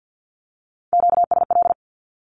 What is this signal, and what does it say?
Morse code, heard as a keyed tone, about 50 wpm, 700 Hz. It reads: QSL
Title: Q-Codes with Spoken Meaning - 50wpm